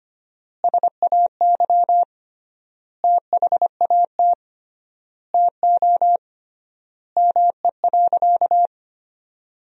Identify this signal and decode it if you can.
Morse code, heard as a keyed tone, about 25 wpm, 700 Hz. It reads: SAY THAT TO ME.
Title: 3x Word-Spacing - Sentences from Top 300 Words - 25wpm